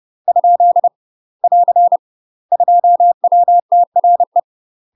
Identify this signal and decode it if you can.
Morse code, heard as a keyed tone, about 30 wpm, 700 Hz. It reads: ? <AR> 2WTRE
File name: UK MASTER QSOs TRAINING / 30WPM / UK MASTER QSOs 0013